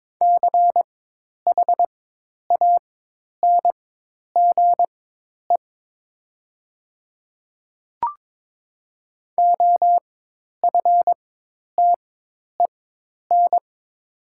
Morse code, heard as a keyed tone, about 23 wpm, spaced as for 10 wpm, 700 Hz. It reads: CHANGE OFTEN
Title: 4x Character Spacing - Top 300 Words - 22wpm